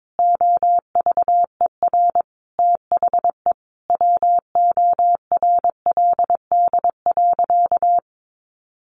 Morse code, heard as a keyed tone, about 22 wpm, 700 Hz. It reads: OVER THE WORLD.
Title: Additional Easy Phrases 22wpm